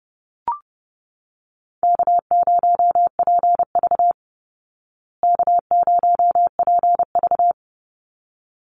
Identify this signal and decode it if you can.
Morse code, heard as a keyed tone, about 30 wpm, 700 Hz. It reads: K0PV K0PV
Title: Easy Call Signs Repeated 2x - 30wpm